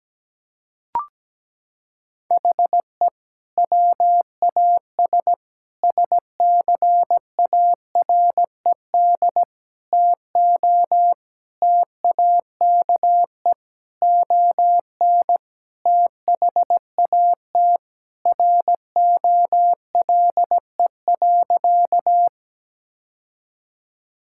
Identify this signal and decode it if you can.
Morse code, heard as a keyed tone, about 17 wpm, 700 Hz. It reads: HE WAS SCARED TO TAKE ON THAT ROLE.
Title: Sentences from Top 1000 Words - Review Part 6 - 17wpm